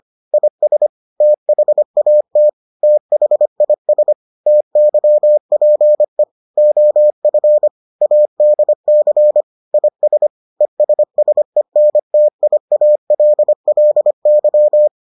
Morse code, heard as a keyed tone, about 25 wpm, 600 Hz. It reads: IS THAT THIS TYPE OF ADC IS ESSENTIALLY